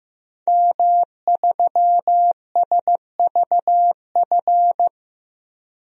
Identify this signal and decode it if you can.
Morse code, heard as a keyed tone, about 15 wpm, 700 Hz. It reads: M3SVF
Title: DXCC-100 Call Signs - 5 Characters - Repeated 2x - 15wpm